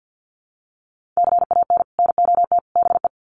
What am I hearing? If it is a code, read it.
Morse code, heard as a keyed tone, about 50 wpm, 700 Hz. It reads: CAN NOT BE
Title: Sets of 3 Words 50wpm